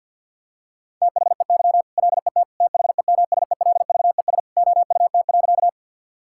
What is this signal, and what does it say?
Morse code, heard as a keyed tone, about 52 wpm, 700 Hz. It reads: THEY LET THEMSELVES OUT.
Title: Sentences from Top 1000 Words - Review Part 5 - 50wpm